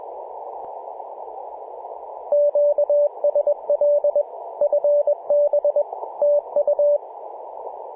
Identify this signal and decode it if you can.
Morse code, heard as a keyed tone, about 20 wpm, 580 Hz, with radio noise. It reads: QSL FB TU
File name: posts / mar-15-0047z-a35jr-28-035mhz-cw